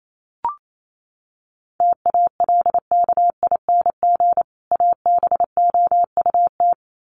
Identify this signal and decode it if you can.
Morse code, heard as a keyed tone, about 28 wpm, 700 Hz. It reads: TALKING ABOUT
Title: Sets of 2 Words 28wpm